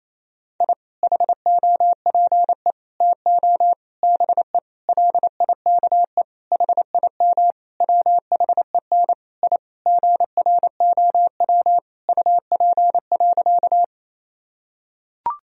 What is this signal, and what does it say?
Morse code, heard as a keyed tone, about 28 wpm, 700 Hz. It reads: I HOPE TO BE LIKE HIM WHEN I GROW UP.
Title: Sentences from Top 1000 Words - Review Part 1 - 28wpm